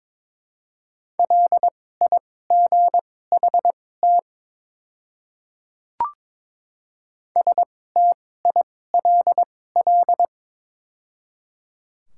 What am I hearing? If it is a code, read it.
Morse code, heard as a keyed tone, about 22 wpm, spaced as for 16 wpm, 700 Hz. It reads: LIGHT STILL
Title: 2x Character Spacing - Top 300 Words - 22wpm